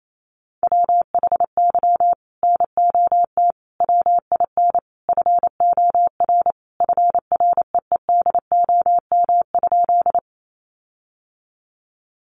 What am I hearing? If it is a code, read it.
Morse code, heard as a keyed tone, about 28 wpm, 700 Hz. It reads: WHY NOT WIN FOR FREEDOM?
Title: Sentences from Top 1500 Words 28wpm